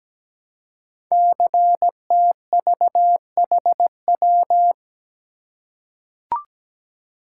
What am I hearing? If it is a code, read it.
Morse code, heard as a keyed tone, about 17 wpm, 700 Hz. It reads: CTVHW
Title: Code Groups - 5 Letters - 17wpm